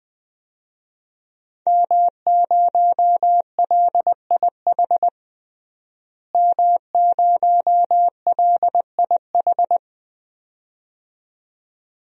Morse code, heard as a keyed tone, about 20 wpm, 700 Hz. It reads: M0LIH M0LIH
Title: DXCC-100 Call Signs - 5 Characters - Repeated 2x - 20wpm